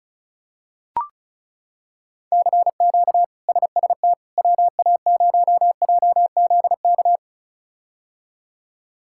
Morse code, heard as a keyed tone, about 35 wpm, 700 Hz. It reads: CQ SST WA0JZK
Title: K1USN SST QSO Practice - 35wpm